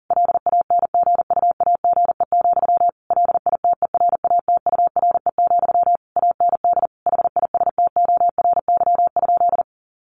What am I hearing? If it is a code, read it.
Morse code, heard as a keyed tone, about 37 wpm, 700 Hz. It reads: LANGUAGE, LITERATURE, AND HISTORY?
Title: Sentences from Top 600 Words 40wpm